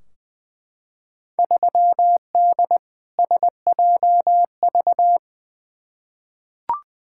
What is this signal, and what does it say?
Morse code, heard as a keyed tone, about 20 wpm, 700 Hz. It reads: 3D SJV
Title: Field Day Exchanges - 20wpm